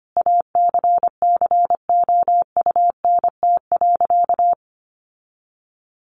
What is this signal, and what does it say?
Morse code, heard as a keyed tone, about 25 wpm, 700 Hz. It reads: ACCOUNT.
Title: Sentences from Top 1400 Words 25wpm